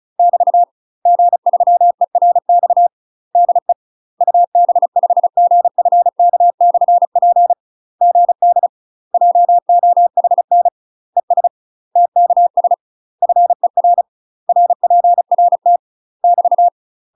Morse code, heard as a keyed tone, about 36 wpm, 700 Hz. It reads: <BT> G3ERX DE UB5GFK/P GD JOHN ES TKS FER RPRT <BT>
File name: UK MASTER QSOs TRAINING / 35WPM / UK MASTER QSOs 0041